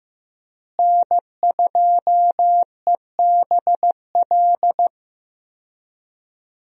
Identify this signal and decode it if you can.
Morse code, heard as a keyed tone, about 16 wpm, 700 Hz. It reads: N2EBL
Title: USA Call Signs - 5 Characters - Repeated 2x - 15wpm